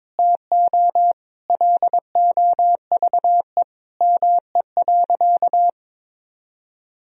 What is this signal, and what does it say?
Morse code, heard as a keyed tone, about 22 wpm, 700 Hz. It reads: TO LOVE ME.
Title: Sentences from Top 400 Words and Abbreviations 22wpm